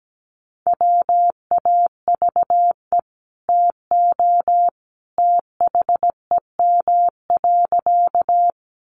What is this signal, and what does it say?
Morse code, heard as a keyed tone, about 17 wpm, 700 Hz. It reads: WAVE TO THEM.
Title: Sentences from Top 1400 Words 17wpm